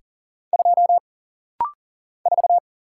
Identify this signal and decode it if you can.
Morse code, heard as a keyed tone, about 40 wpm, 700 Hz. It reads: 2 4
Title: Single Letter-Number Alt - Rapid-Fire 40wpm